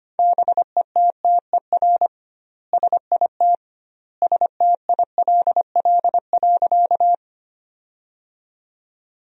Morse code, heard as a keyed tone, about 25 wpm, 700 Hz. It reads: BETTER SIT STILL.
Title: 2x Word-Spacing - Sentences from Top 500 Words - 25wpm